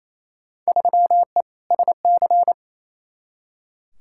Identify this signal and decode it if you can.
Morse code, heard as a keyed tone, about 28 wpm, 700 Hz. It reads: 3E SC
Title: Field Day Exchanges Encore - 28wpm